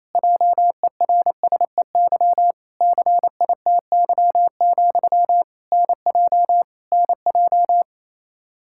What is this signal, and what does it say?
Morse code, heard as a keyed tone, about 28 wpm, 700 Hz. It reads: JERSEY CITY, NJ NJ
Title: Practice QSO - Exchange 1 - Beginning - 28wpm